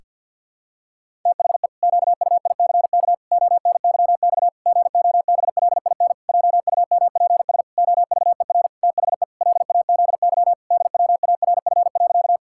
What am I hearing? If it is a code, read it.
Morse code, heard as a keyed tone, about 50 wpm, 700 Hz. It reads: THE QUICK ONYX GOBLIN JUMPS OVER THE LAZY DWARF.
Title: Pangram Challenge v2 - 50wpm